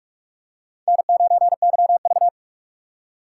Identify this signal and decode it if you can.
Morse code, heard as a keyed tone, about 46 wpm, 700 Hz. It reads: N9YV
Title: USA Call Signs - 2x Character Spacing - 45wpm